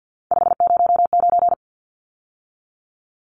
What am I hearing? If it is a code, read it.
Morse code, heard as a keyed tone, about 50 wpm, 700 Hz. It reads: 509
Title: Numbers - 3-Digit 50wpm